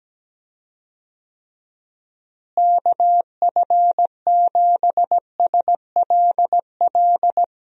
Morse code, heard as a keyed tone, about 17 wpm, 700 Hz. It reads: KF7SLL
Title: USA Call Signs - 6 Characters - Repeated 2x - 17wpm